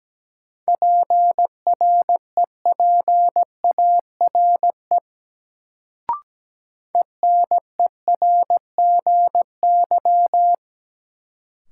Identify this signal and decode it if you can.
Morse code, heard as a keyed tone, about 17 wpm, 700 Hz. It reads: PREPARE ENERGY
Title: Sentences from Top 900 Words - New Word Drill - 17wpm